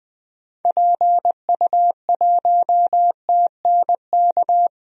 Morse code, heard as a keyed tone, about 20 wpm, 700 Hz. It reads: PU1TNK
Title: DXCC-100 Call Signs - 6 Characters - Repeated 3x - 20wpm